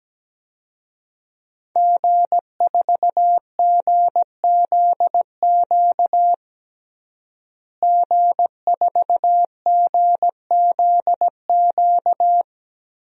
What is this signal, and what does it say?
Morse code, heard as a keyed tone, about 17 wpm, 700 Hz. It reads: G4GZQ G4GZQ
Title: DXCC-100 Call Signs - 5 Characters - Repeated 3x - 17wpm